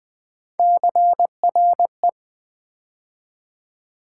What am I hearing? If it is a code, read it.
Morse code, heard as a keyed tone, about 20 wpm, 700 Hz. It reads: CRE
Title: Build A Word - 3 Letters At A Time - 6-Letter Words - 20wpm